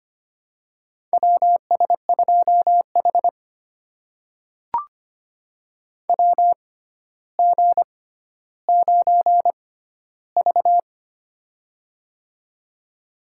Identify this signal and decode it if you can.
Morse code, heard as a keyed tone, about 25 wpm, spaced as for 13 wpm, 700 Hz. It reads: WS2H WG9V
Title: USA Call Signs - 4 Characters - 6x Character Spacing - 25wpm